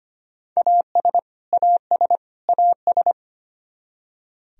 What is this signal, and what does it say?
Morse code, heard as a keyed tone, about 25 wpm, 700 Hz. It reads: AS AS AS
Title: Top 100 Words Repeated 25wpm